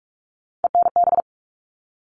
Morse code, heard as a keyed tone, about 44 wpm, 700 Hz. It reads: ENB
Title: All Cut Numbers - 3-Digit 45wpm